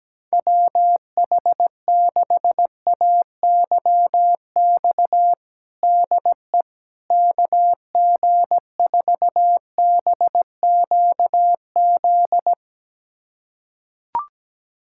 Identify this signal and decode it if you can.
Morse code, heard as a keyed tone, about 17 wpm, 700 Hz. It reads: WH6AYX DE KG4BQZ
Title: USA Callsign DE Callsign 17wpm